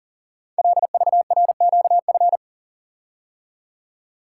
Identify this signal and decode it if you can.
Morse code, heard as a keyed tone, about 42 wpm, 690 Hz. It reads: LVRQF
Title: Code Groups - 5 Letters - 40wpm